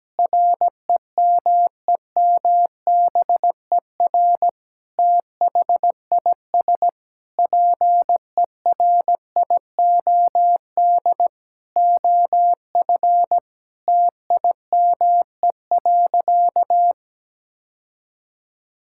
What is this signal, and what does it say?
Morse code, heard as a keyed tone, about 17 wpm, 700 Hz. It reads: REMEMBER THIS PERIOD OF TIME.
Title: Sentences from Top 1000 Words - Review Part 5 - 17wpm